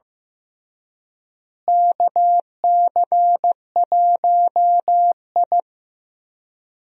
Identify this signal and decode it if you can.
Morse code, heard as a keyed tone, about 15 wpm, 700 Hz. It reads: KC1I
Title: USA Call Signs - 4 Characters - Repeated 3x - 15wpm